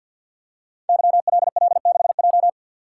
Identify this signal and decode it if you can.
Morse code, heard as a keyed tone, about 48 wpm, 700 Hz. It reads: YLL6J
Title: Code Groups - 5 Letter:Numbers - 50wpm